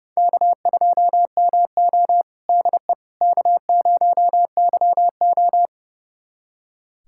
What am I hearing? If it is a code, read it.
Morse code, heard as a keyed tone, about 30 wpm, 700 Hz. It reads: K2MO DE K0YO
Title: USA Callsign DE Callsign - 4 Characters - 30wpm